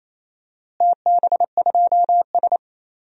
Morse code, heard as a keyed tone, about 27 wpm, 700 Hz. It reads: TB2S
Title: DXCC-100 Call Signs - 4 Characters - Repeated 3x - 28wpm